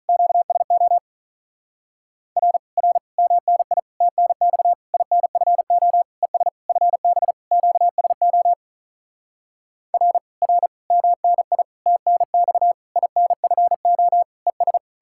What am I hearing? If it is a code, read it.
Morse code, heard as a keyed tone, about 48 wpm, 700 Hz. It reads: QSO R R MNI TNX INFO ES FB QSO R R MNI TNX INFO ES